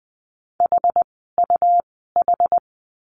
Morse code, heard as a keyed tone, about 20 wpm, spaced as for 14 wpm, 700 Hz. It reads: HUH
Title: 101 Lesson 17f - Letter u - Words 20wpm